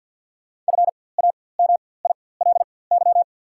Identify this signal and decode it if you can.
Morse code, heard as a keyed tone, about 48 wpm, spaced as for 23 wpm, 700 Hz. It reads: FAMILY
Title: 4x Character Spacing - Top 500 Words - 50wpm